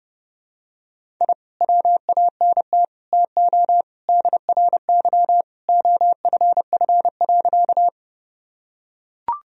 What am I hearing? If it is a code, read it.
Morse code, heard as a keyed tone, about 30 wpm, 700 Hz. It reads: I WANT TO DRY OFF.
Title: Sentences from Top 1000 Words - Review Part 3 - 30wpm